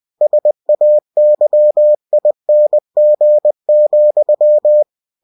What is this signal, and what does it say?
Morse code, heard as a keyed tone, about 20 wpm, 600 Hz. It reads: SAYING,
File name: cw-books / winnie-the-pooh / winnie-20wpm-page-0091